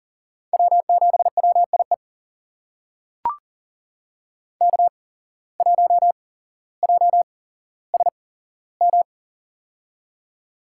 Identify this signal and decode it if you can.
Morse code, heard as a keyed tone, about 40 wpm, spaced as for 17 wpm, 700 Hz. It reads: W7WIE K1JSM
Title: USA Call Signs - 5 Characters - 8x Character Spacing - 40wpm